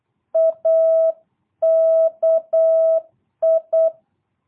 Morse code, heard as a keyed tone, about 8 wpm, 660 Hz. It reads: AKI
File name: jh8cbh / 2022cw